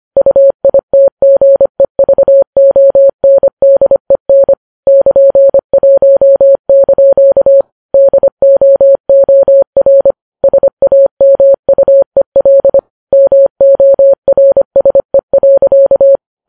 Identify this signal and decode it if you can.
Morse code, heard as a keyed tone, about 25 wpm, 550 Hz. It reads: UITGEVONDEN <KN>1) DOOR SAMUEL MORSE.